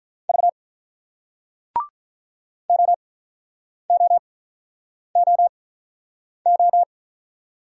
Morse code, heard as a keyed tone, about 47 wpm, spaced as for 9 wpm, 690 Hz. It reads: V OOOO